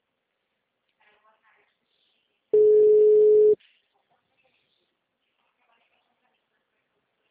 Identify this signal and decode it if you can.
Ringback tone (Europe, 425 Hz), with speech.
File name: out- -2016- - - .191